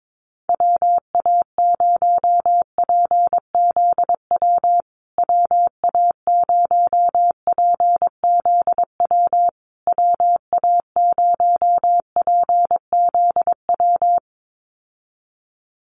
Morse code, heard as a keyed tone, about 22 wpm, 700 Hz. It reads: WA0PZW WA0PZW WA0PZW
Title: USA Call Signs Repeated 3x 22wpm